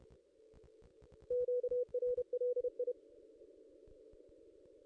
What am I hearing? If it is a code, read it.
Morse code, heard as a keyed tone, about 31 wpm, 490 Hz. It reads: QRLI
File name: contests / ss-cw-k5tr-op-n5rz -1600